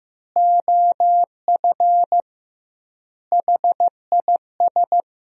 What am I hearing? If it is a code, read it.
Morse code, heard as a keyed tone, about 16 wpm, 700 Hz. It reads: OF HIS
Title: 2x Word-Spacing - Sentences from Top 300 Words - 15wpm